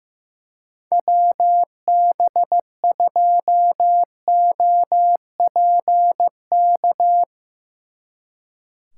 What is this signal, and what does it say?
Morse code, heard as a keyed tone, about 15 wpm, 700 Hz. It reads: WB2OPK
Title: Speed Racing - USA Call Signs 15wpm